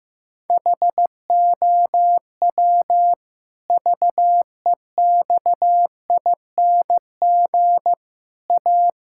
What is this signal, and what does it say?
Morse code, heard as a keyed tone, about 15 wpm, 700 Hz. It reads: HOW VEXING A
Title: Pangram Challenge v2 - 15wpm